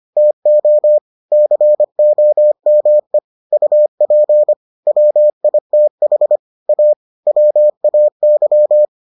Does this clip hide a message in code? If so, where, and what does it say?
Yes: Morse code from 0.2 to 8.9 s, heard as a keyed tone, about 25 wpm, 600 Hz. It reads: TO COME UP WITH A WAY